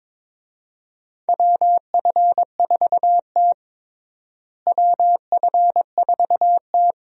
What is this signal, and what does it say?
Morse code, heard as a keyed tone, about 22 wpm, 700 Hz. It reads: WF4T WF4T
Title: USA Call Signs - 4 Characters - Repeated 3x - 22wpm